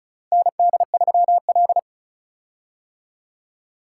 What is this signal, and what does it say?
Morse code, heard as a keyed tone, about 36 wpm, 700 Hz. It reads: ND3L
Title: K1USN SST QSO Practice - 35wpm